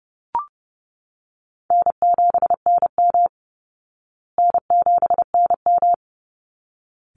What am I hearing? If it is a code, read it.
Morse code, heard as a keyed tone, about 30 wpm, 700 Hz. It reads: N7NM N7NM
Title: Call Signs Repeated 3x - 30wpm